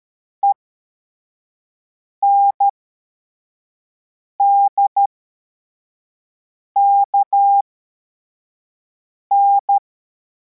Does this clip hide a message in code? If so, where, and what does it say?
Yes: Morse code from 0.4 to 9.8 s, heard as a keyed tone, about 13 wpm, spaced as for 4 wpm, 800 Hz. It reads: ENDKN